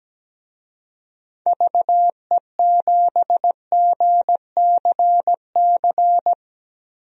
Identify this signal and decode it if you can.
Morse code, heard as a keyed tone, about 17 wpm, 700 Hz. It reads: VE7GCC
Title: DXCC-100 Call Signs - 6 Characters - 17wpm